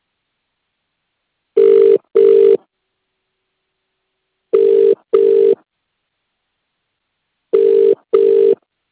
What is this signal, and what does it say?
Ringback tone (UK), 400 + 450 Hz.